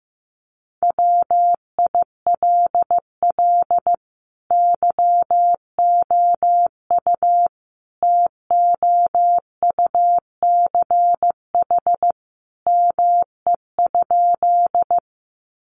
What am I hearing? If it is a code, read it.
Morse code, heard as a keyed tone, about 15 wpm, 700 Hz. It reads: WILL YOU TOUCH ME?
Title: Sentences from Top 1200 Words 15wpm